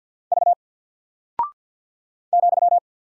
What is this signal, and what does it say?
Morse code, heard as a keyed tone, about 49 wpm, 700 Hz. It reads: V ,
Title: Single Character (Letter:Number:Punctuation:Prosign) 50wpm